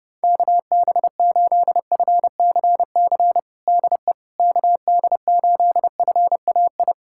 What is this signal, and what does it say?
Morse code, heard as a keyed tone, about 32 wpm, 700 Hz. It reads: KB8FCC DE KD8FAI
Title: USA Callsign DE Callsign 30wpm